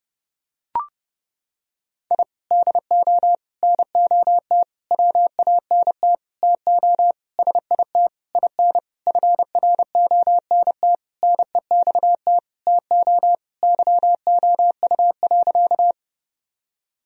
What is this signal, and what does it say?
Morse code, heard as a keyed tone, about 30 wpm, 700 Hz. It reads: I DO NOT WANT TO SIT IN FRONT NEXT TO YOU.
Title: Sentences from Top 1000 Words - Review Part 2 - 30wpm